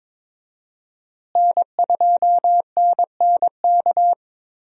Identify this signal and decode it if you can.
Morse code, heard as a keyed tone, about 22 wpm, 700 Hz. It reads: N2NNK
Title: USA Call Signs - 5 Characters - Repeated 2x - 22wpm